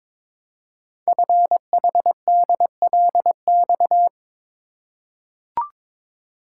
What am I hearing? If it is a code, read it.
Morse code, heard as a keyed tone, about 22 wpm, 700 Hz. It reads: FHDLX
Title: Code Groups - 5 Letter:Numbers - 22wpm